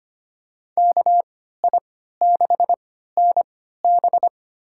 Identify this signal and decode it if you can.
Morse code, heard as a keyed tone, about 25 wpm, spaced as for 14 wpm, 700 Hz. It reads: KI6NB
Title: USA Call Signs - 3x Character Spacing - 25wpm